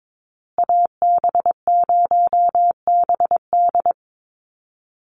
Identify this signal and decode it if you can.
Morse code, heard as a keyed tone, about 22 wpm, 700 Hz. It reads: AB0BD
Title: USA Call Signs Repeated 2x 22wpm